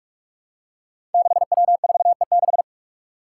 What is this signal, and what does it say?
Morse code, heard as a keyed tone, about 44 wpm, 700 Hz. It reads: 6W4E6
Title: Five Challenging Letter-Number Combinations 45wpm